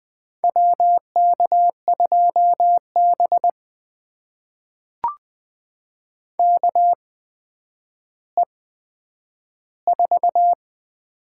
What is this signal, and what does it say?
Morse code, heard as a keyed tone, about 20 wpm, spaced as for 10 wpm, 700 Hz. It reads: WK2B KE4
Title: USA Call Signs - 8x Character Spacing - 20wpm